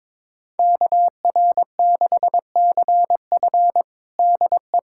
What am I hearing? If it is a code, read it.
Morse code, heard as a keyed tone, about 22 wpm, 700 Hz. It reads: KR6CF DE
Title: USA Callsign DE Callsign - 5 Characters - 22wpm